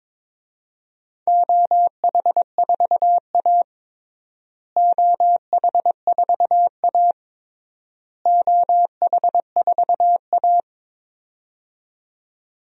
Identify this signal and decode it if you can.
Morse code, heard as a keyed tone, about 22 wpm, 700 Hz. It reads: OH4A OH4A OH4A
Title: DXCC-100 Call Signs - 4 Characters - Repeated 3x - 22wpm